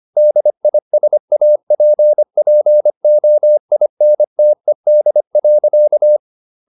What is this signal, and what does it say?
Morse code, heard as a keyed tone, about 25 wpm, 600 Hz. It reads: DISAPPOINTED.